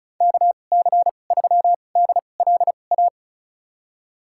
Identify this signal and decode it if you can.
Morse code, heard as a keyed tone, about 34 wpm, spaced as for 25 wpm, 700 Hz. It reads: KC3DLA
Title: USA Call Signs - 6 Characters - 2x Character Spacing - 35wpm